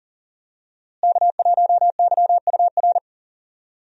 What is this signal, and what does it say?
Morse code, heard as a keyed tone, about 37 wpm, 700 Hz. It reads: K1YUR
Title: USA Call Signs - 5 Characters - Repeated 3x - 40wpm